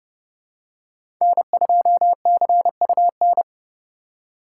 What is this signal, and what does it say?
Morse code, heard as a keyed tone, about 30 wpm, 700 Hz. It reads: N2CUN
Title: USA Call Signs - 5 Characters - 4x Character Spacing - 30wpm